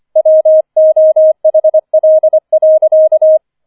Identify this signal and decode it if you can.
Morse code, heard as a keyed tone, about 25 wpm, 630 Hz. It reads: WOHL.
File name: 005 Fünfter Streich 25wpm